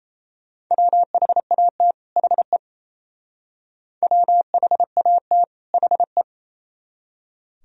Morse code, heard as a keyed tone, about 36 wpm, 700 Hz. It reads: WHAT HE WHAT HE